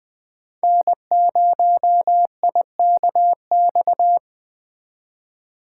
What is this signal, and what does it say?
Morse code, heard as a keyed tone, about 20 wpm, 700 Hz. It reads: N0IKX
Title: Speed Racing - USA 1x3 Call Signs 17wpm